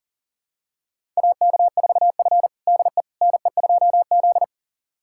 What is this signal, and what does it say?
Morse code, heard as a keyed tone, about 40 wpm, 700 Hz. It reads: AK4F DE NE2Z
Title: USA Callsign DE Callsign - 4 Characters - 40wpm